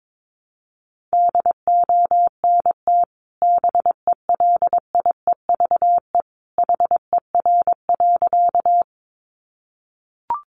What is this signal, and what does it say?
Morse code, heard as a keyed tone, about 22 wpm, 700 Hz. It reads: DONT BELIEVE HER.
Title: Sentences from Top 1000 Words 22wpm